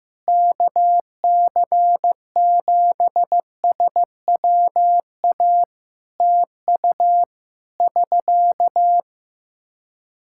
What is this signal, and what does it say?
Morse code, heard as a keyed tone, about 15 wpm, 700 Hz. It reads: KC7SWA TU <SK>
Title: Practice QSO - Exchange 4 - Ending - 15wpm